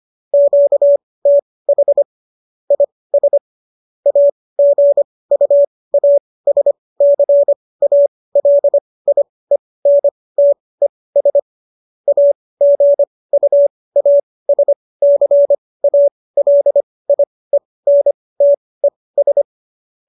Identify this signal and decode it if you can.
Morse code, heard as a keyed tone, about 25 wpm, spaced as for 18 wpm, 570 Hz. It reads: QTH IS AGUASCALIENTES AGUASCALIENTES
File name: qso204 18